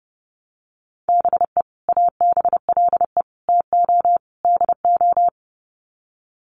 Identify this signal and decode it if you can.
Morse code, heard as a keyed tone, about 30 wpm, 700 Hz. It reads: BE ABLE TO DO
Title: Sets of 4 Words 30wpm